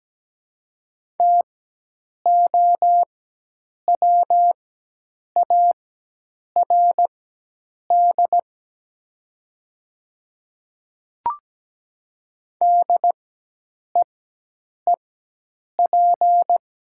Morse code, heard as a keyed tone, about 17 wpm, spaced as for 8 wpm, 700 Hz. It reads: TOWARD DEEP
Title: 4x Character Spacing - Top 500 Words - 17wpm